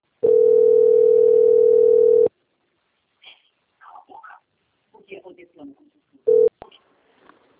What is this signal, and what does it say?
Ringback tone (North America), 440 + 480 Hz, with speech.
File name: out- - - - - .8028